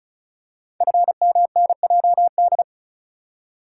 Morse code, heard as a keyed tone, about 36 wpm, 700 Hz. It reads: FMNJD
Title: Code Groups - 5 Letters - 35wpm